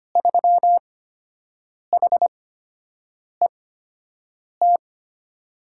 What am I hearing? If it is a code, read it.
Morse code, heard as a keyed tone, about 25 wpm, spaced as for 7 wpm, 700 Hz. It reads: 3HET
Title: USA Call Signs - 8x Character Spacing - 25wpm